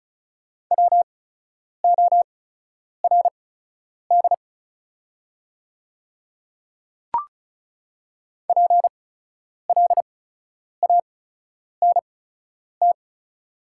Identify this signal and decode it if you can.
Morse code, heard as a keyed tone, about 36 wpm, spaced as for 10 wpm, 700 Hz. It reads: WORD PLANT
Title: 8x Character Spacing - Top 300 Words - 35wpm